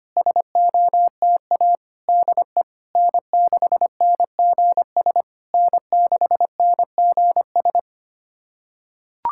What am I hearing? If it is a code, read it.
Morse code, heard as a keyed tone, about 25 wpm, 700 Hz. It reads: SOTA DE N6NGS N6NGS
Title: SOTA QSO Practice 25wpm